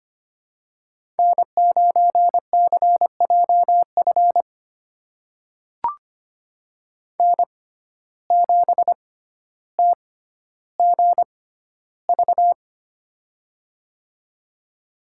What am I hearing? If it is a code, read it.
Morse code, heard as a keyed tone, about 25 wpm, spaced as for 13 wpm, 700 Hz. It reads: N9CJF N7TGV
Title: USA Call Signs - 5 Characters - 6x Character Spacing - 25wpm